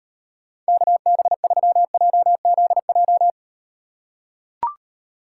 Morse code, heard as a keyed tone, about 38 wpm, 700 Hz. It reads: KB3JZJ